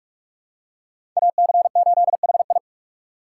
Morse code, heard as a keyed tone, about 44 wpm, 700 Hz. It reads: AC8HI
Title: USA Call Signs - 5 Characters - 3x Character Spacing - 45wpm